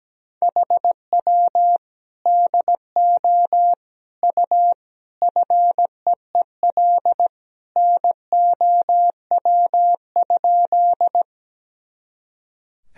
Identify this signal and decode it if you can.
Morse code, heard as a keyed tone, about 17 wpm, 700 Hz. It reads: HW DO U FEEL NOW?
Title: Sentences from Top 400 Words and Abbreviations 17wpm